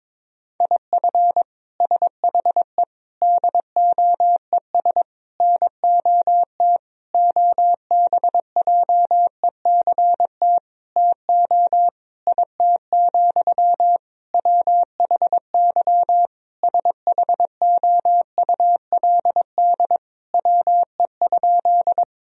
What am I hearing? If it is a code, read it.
Morse code, heard as a keyed tone, about 22 wpm, 700 Hz. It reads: IF SHE DOES NOT OBJECT TO IT, WHY SHOULD WE?
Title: Sentences from Top 1000 Words - Review Part 3 - 22wpm